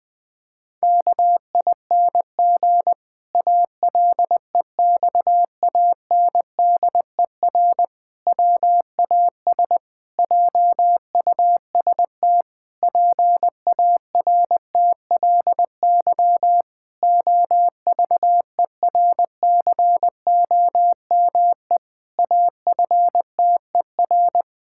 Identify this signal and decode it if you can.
Morse code, heard as a keyed tone, about 20 wpm, 700 Hz. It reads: KING ALEXANDER WAS JUST PARTLY OVERCOME AFTER
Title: Pangram Challenge v2 - 20wpm